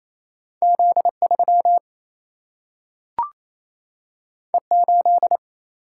Morse code, heard as a keyed tone, about 28 wpm, 700 Hz. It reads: Z3 E8
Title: Single-Letter Single-Number 28wpm